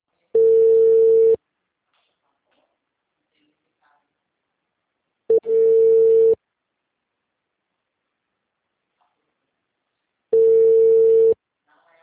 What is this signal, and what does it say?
Ringback tone (China, 450 Hz), with speech.